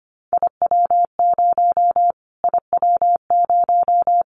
Morse code, heard as a keyed tone, about 24 wpm, 700 Hz. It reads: IW0 IW0
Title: Callsign Prefixes for Top 25 DXCC - Repeated 3x - 25wpm